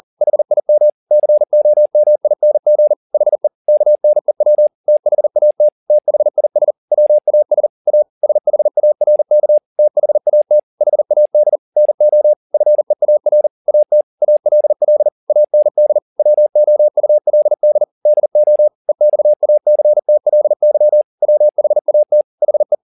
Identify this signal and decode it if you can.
Morse code, heard as a keyed tone, about 40 wpm, 600 Hz. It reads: HIM COMING HE KNEW THAT THIS WAS A SHARK THAT HAD NO FEAR AT ALL AND WOULD DO EXACTLY WHAT HE
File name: the old man and the sea 2 40wpm